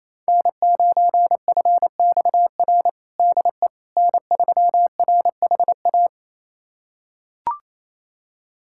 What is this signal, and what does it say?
Morse code, heard as a keyed tone, about 28 wpm, 700 Hz. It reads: N9FXR DE N3RHA
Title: USA Callsign DE Callsign - 5 Characters - 28wpm